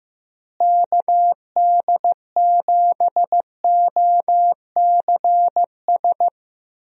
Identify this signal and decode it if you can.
Morse code, heard as a keyed tone, about 15 wpm, 700 Hz. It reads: KD7OCS
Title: USA Call Signs - 6 Characters - Repeated 3x - 15wpm